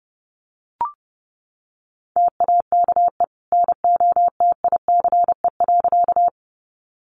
Morse code, heard as a keyed tone, about 30 wpm, 700 Hz. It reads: TAKE NOTICE.
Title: Sentences from Top 400 Words 30wpm